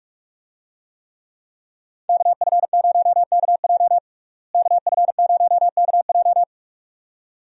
Morse code, heard as a keyed tone, about 46 wpm, 700 Hz. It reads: KF0KJ KF0KJ
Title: USA Call Signs - 5 Characters - Repeated 2x - 45wpm